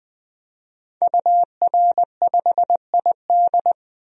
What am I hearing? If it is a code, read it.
Morse code, heard as a keyed tone, about 20 wpm, 700 Hz. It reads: UR5ID
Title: DXCC-100 Call Signs - 5 Characters - Repeated 3x - 20wpm